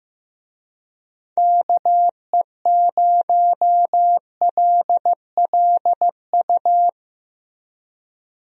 Morse code, heard as a keyed tone, about 15 wpm, 700 Hz. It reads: KE0LLU
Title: USA Call Signs - 6 Characters - Repeated 3x - 15wpm